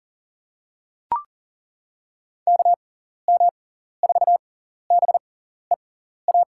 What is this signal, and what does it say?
Morse code, heard as a keyed tone, about 40 wpm, spaced as for 14 wpm, 700 Hz. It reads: KM4BEA
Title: USA Call Signs - 6 Characters - 6x Character Spacing - 40wpm